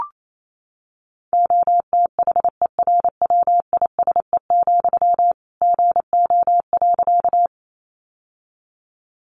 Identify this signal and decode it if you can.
Morse code, heard as a keyed tone, about 28 wpm, 700 Hz. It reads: OTHERWISE, GO.
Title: Sentences from Top 1200 Words 28wpm